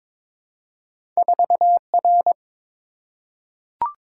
Morse code, heard as a keyed tone, about 22 wpm, 700 Hz. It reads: 4R
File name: practice / 22wpm / Two Letter-Number Combinations - 22wpm